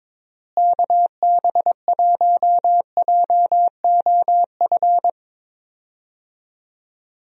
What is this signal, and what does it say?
Morse code, heard as a keyed tone, about 22 wpm, 700 Hz. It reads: KB1JOF
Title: USA Call Signs - 6 Characters - Repeated 2x - 22wpm